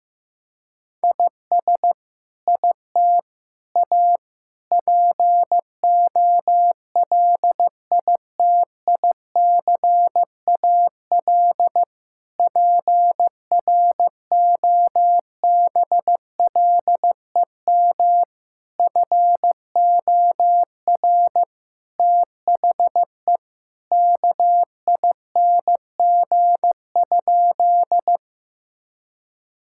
Morse code, heard as a keyed tone, about 15 wpm, 700 Hz. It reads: IS IT A POLITICAL PROBLEM FOR THE KING?
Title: Sentences from Top 1000 Words - Review Part 5 - 15wpm